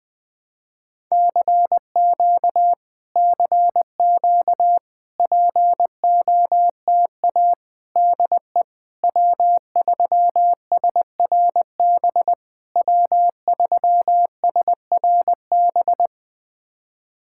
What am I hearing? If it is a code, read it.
Morse code, heard as a keyed tone, about 20 wpm, 700 Hz. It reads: CQ CQ POTA DE W3SRB W3SRB
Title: POTA QSO Practice - 20wpm